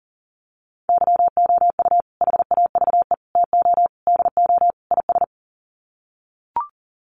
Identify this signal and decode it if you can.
Morse code, heard as a keyed tone, about 40 wpm, 700 Hz. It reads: YOU HAVE TO DO IS
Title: Sets of 5 Words 40wpm